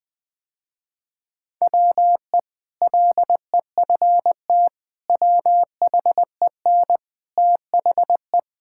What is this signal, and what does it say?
Morse code, heard as a keyed tone, about 21 wpm, 700 Hz. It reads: WE LEFT WHEN THE
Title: Sentences with 2-Word Phrases - Spoken Beginning - 20wpm